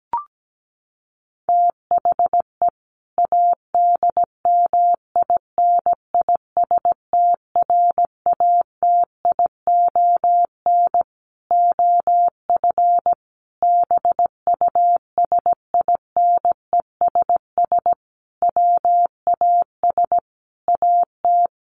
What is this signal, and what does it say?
Morse code, heard as a keyed tone, about 17 wpm, 700 Hz. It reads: THE ADMINISTRATION OF BUSINESS WAS AT
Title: Sentences from Top 1300 Words 17wpm